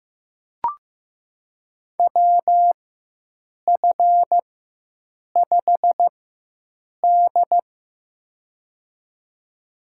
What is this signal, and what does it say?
Morse code, heard as a keyed tone, about 15 wpm, spaced as for 7 wpm, 700 Hz. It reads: WF5D
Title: USA Call Signs - 4 Characters - 4x Character Spacing - 15wpm